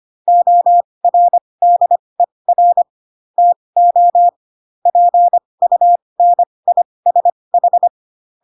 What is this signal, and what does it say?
Morse code, heard as a keyed tone, about 25 wpm, spaced as for 20 wpm, 700 Hz. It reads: ORDER TO PUNISH
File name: Pinocchio / Pinocchio ch8 15 25 ik1hge